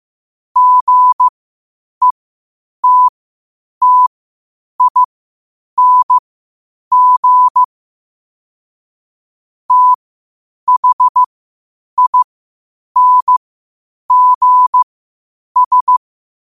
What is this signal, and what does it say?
Morse code, heard as a keyed tone, about 15 wpm, spaced as for 8 wpm, 1000 Hz. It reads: GETTING THINGS
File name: qotdfiles / 8wpm